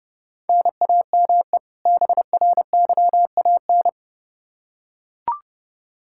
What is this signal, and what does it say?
Morse code, heard as a keyed tone, about 32 wpm, 700 Hz. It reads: NAME BRYAN
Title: K1USN SST QSO Practice 30wpm